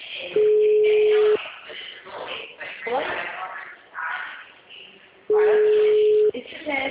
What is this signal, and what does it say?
Ringback tone (Europe, 425 Hz), with speech.